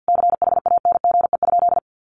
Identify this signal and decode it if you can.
Morse code, heard as a keyed tone, about 50 wpm, 700 Hz. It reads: CHANGE?
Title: Sentences from Top 1500 Words 50wpm